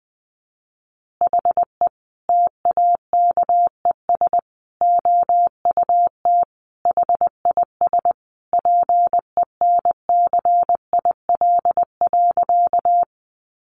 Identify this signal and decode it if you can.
Morse code, heard as a keyed tone, about 20 wpm, 700 Hz. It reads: HE TAKES OUT HIS PENCIL.
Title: Additional Easy Phrases 20wpm